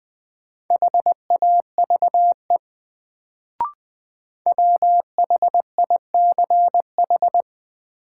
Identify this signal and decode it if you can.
Morse code, heard as a keyed tone, about 20 wpm, 700 Hz. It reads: HAVE WHICH
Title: Sets of 2 Words - New Word Drill - 20wpm